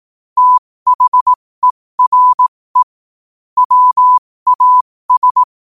Morse code, heard as a keyed tone, about 18 wpm, spaced as for 15 wpm, 1000 Hz. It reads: THERE WAS
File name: 15wpm